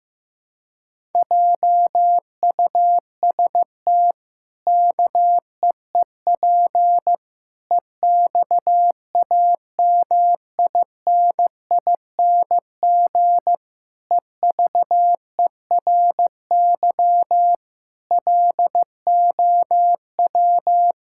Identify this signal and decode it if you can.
Morse code, heard as a keyed tone, about 15 wpm, 700 Hz. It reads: JUST KEEP EXAMINING EVERY LOW
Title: Pangram Challenge v2 - 15wpm